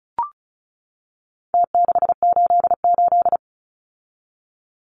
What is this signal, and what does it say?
Morse code, heard as a keyed tone, about 34 wpm, 700 Hz. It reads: T688
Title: Numbers - Contest Style - 4-Digit 35wpm